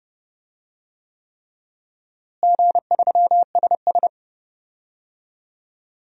Morse code, heard as a keyed tone, about 30 wpm, 700 Hz. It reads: G3SS
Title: DXCC-100 Call Signs - 4 Characters - Repeated 3x - 30wpm